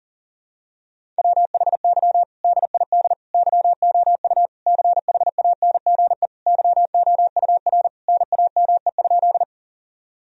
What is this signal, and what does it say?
Morse code, heard as a keyed tone, about 37 wpm, 700 Hz. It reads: WHY DID YOU CHANGE YOUR NAME?
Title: Sentences from Top 1000 Words - Review Part 1 - 40wpm